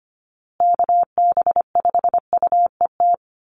Morse code, heard as a keyed tone, about 25 wpm, 700 Hz. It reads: KB5UET
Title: USA Call Signs Repeated 3x 25wpm